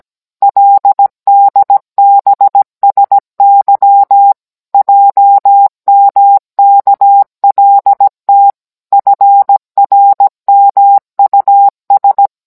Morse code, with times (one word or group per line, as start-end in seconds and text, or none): 0.4-4.3 LDBSY
4.7-8.5 JMKLT
8.9-12.3 FRMUS